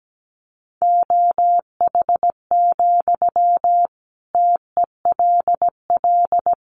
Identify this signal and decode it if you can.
Morse code, heard as a keyed tone, about 17 wpm, 700 Hz. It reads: OH, TELL
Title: Sentences from Top 500 Words 17wpm